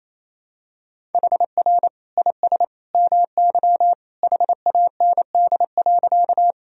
Morse code, heard as a keyed tone, about 28 wpm, 700 Hz. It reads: HR IS MY HAND.
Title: Sentences from Top 200 Words and Abbreviations 28wpm